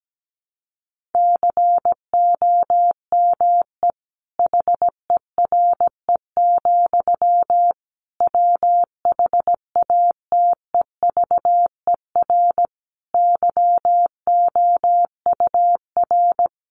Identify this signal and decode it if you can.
Morse code, heard as a keyed tone, about 17 wpm, 700 Hz. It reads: COME HERE, WHATEVER YOUR
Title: Sentences from Top 1100 Words 17wpm